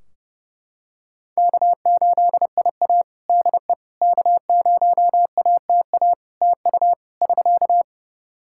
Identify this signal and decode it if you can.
Morse code, heard as a keyed tone, about 30 wpm, 700 Hz. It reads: K8IA DE K0ATA TU <SK>
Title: Practice QSO - Exchange 4 - Ending - 30wpm